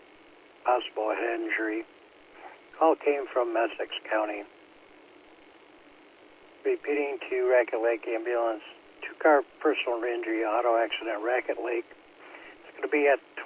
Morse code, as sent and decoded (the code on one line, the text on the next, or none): none